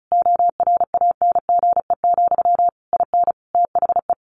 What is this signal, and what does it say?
Morse code, heard as a keyed tone, about 34 wpm, 700 Hz. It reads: ORANGE, IN THE
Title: A Sound Of Thunder 35wpm